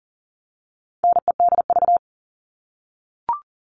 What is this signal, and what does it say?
Morse code, heard as a keyed tone, about 40 wpm, 700 Hz. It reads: NEDV
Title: All Cut Numbers - 4-Digit 40wpm